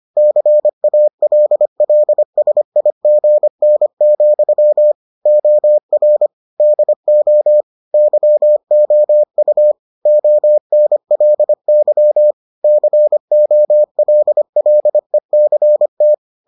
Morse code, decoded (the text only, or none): CALLSIGN, OR DO YOU ONLY COLLECT